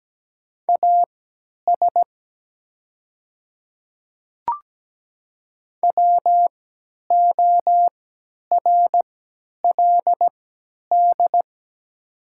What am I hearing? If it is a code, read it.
Morse code, heard as a keyed tone, about 17 wpm, spaced as for 10 wpm, 700 Hz. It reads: AS WORLD
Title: 3x Character Spacing - Top 200 Words - 17wpm